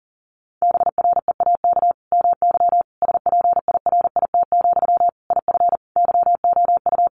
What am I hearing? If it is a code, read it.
Morse code, heard as a keyed tone, about 40 wpm, 700 Hz. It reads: BREAK MY SPIRIT, IF YOU
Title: Sentences from Top 1400 Words 40wpm